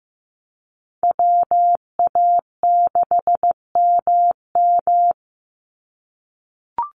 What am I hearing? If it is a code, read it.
Morse code, heard as a keyed tone, about 15 wpm, 700 Hz. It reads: WA6MM
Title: NA SOTA Chaser Callsigns 15wpm